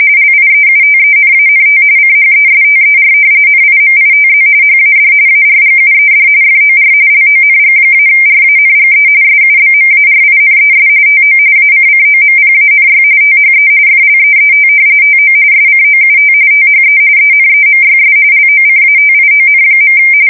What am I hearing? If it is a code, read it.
RTTY, about 45.45 baud. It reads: LIEUTENANT CHESTER NIMITZ HAD ARGUED FOR LONG-RANGE SUBMARINES TO ACCOMPANY THE FLEET TO SCOUT THE ENEMYS LOCATION.
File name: fsk170 / usnavy4